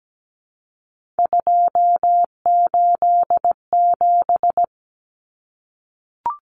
Morse code, heard as a keyed tone, about 17 wpm, 700 Hz. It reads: 287
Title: Numbers - 3-Digit 17wpm